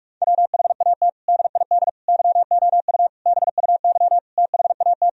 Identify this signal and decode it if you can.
Morse code, heard as a keyed tone, about 46 wpm, 700 Hz. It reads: WHAT DID YOU BUY THAT
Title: Sentences from Top 1000 Words - Review Part 4 - 45wpm